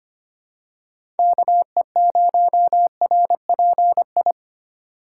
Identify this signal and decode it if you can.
Morse code, heard as a keyed tone, about 25 wpm, 700 Hz. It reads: KE0RPI
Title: USA Call Signs - 6 Characters - 3x Character Spacing - 25wpm